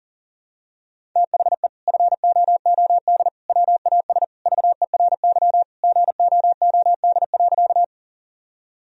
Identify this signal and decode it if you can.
Morse code, heard as a keyed tone, about 40 wpm, 700 Hz. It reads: THE FOOD WAS VERY GOOD.
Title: Sentences from Top 1000 Words - Review Part 1 - 40wpm